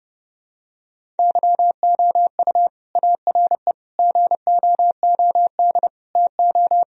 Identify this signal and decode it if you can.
Morse code, heard as a keyed tone, about 32 wpm, 700 Hz. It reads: YOU ARE GOOD TO
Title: Sentences from Top 1000 Words - Review Part 1 - 30wpm